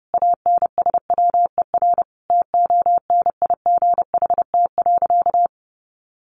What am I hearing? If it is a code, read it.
Morse code, heard as a keyed tone, about 30 wpm, 700 Hz. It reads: ANSWER TONIGHT.
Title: Sentences from Top 600 Words 30wpm